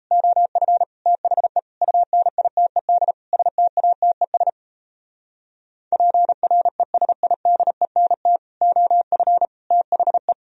Morse code, heard as a keyed tone, about 37 wpm, 700 Hz. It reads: OF THE UNITED STATES PRESIDENT OF THE